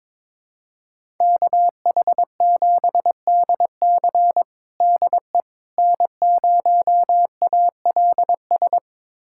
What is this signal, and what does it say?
Morse code, heard as a keyed tone, about 21 wpm, 700 Hz. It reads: KH7DC DE N0ALS
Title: USA Callsign DE Callsign - 5 Characters - 22wpm